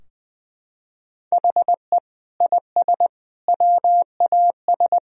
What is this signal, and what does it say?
Morse code, heard as a keyed tone, about 20 wpm, 700 Hz. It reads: HE IS WAS
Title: Sentences from Top 1000 Words - Review Part 2 - 20wpm